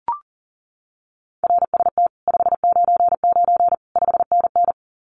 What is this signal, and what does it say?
Morse code, heard as a keyed tone, about 38 wpm, 700 Hz. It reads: RST 599 5NN
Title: Common QSO Phrases Encore 40wpm